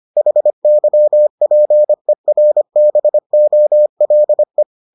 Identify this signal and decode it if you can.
Morse code, heard as a keyed tone, about 25 wpm, 600 Hz. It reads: HYPERBOLE